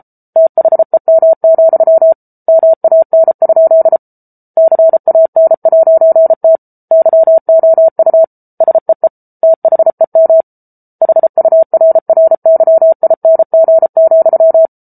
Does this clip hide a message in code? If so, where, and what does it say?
Yes: Morse code from 0.4 to 14.7 s, heard as a keyed tone, about 34 wpm, 650 Hz. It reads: THEM, MAN? CAN'T YOU SEE THEM HURRYING,